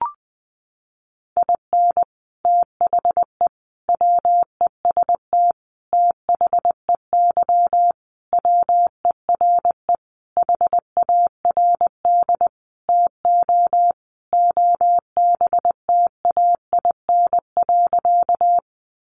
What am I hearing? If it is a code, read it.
Morse code, heard as a keyed tone, about 20 wpm, 700 Hz. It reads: IN THE WEST THEY WERE HARD TO OBTAIN.
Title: Sentences from Top 1300 Words 20wpm